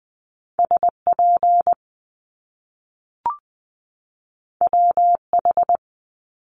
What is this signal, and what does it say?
Morse code, heard as a keyed tone, about 20 wpm, 700 Hz. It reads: SP WH
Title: Top Two Letter Combinations in Words 20wpm